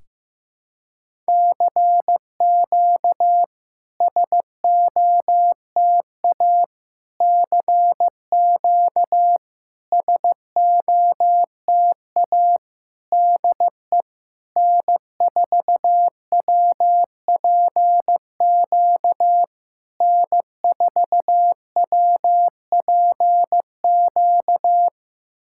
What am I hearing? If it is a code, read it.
Morse code, heard as a keyed tone, about 15 wpm, 700 Hz. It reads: CQ SOTA CQ SOTA DE N4WPQ N4WPQ
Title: SOTA QSO Practice 15wpm